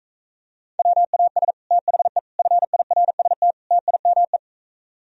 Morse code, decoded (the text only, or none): WAS THE FIRST TIME